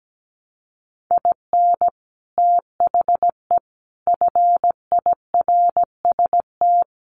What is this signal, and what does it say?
Morse code, heard as a keyed tone, about 17 wpm, 700 Hz. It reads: IN THE FIRST
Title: Sets of 5 Words Encore2 17wpm